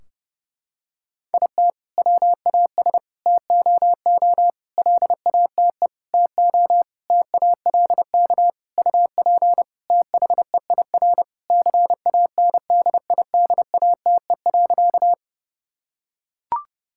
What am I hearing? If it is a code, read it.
Morse code, heard as a keyed tone, about 30 wpm, 700 Hz. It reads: IT WAS TOO LATE TO TALK UP THEIR CANDIDATE.
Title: Sentences from Top 1600 Words 30wpm